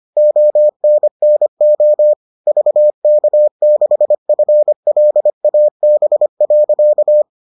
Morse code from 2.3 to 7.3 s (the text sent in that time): VK6FLAB.